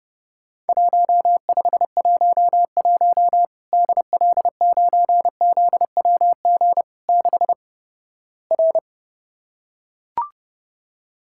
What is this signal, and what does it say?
Morse code, heard as a keyed tone, about 32 wpm, 700 Hz. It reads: 1511 DL9ZWG 6 R
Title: Worked All Europe DX Contest - QTC Exchange - 30wpm